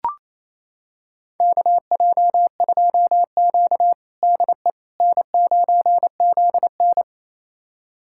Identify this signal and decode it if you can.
Morse code, heard as a keyed tone, about 28 wpm, 700 Hz. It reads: KJ2Q DE N9ZN
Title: USA Callsign DE Callsign - 4 Characters - 28wpm